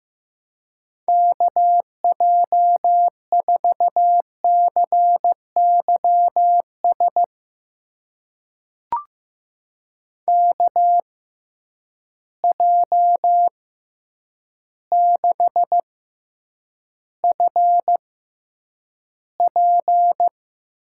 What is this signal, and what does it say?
Morse code, heard as a keyed tone, about 15 wpm, spaced as for 8 wpm, 700 Hz. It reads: KJ4CYS KJ6FP
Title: USA Call Signs - 6 Characters - 6x Character Spacing - 15wpm